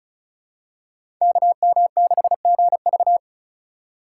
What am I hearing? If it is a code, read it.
Morse code, heard as a keyed tone, about 34 wpm, 700 Hz. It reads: KM6GV
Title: USA Call Signs - 5 Characters - 4x Character Spacing - 35wpm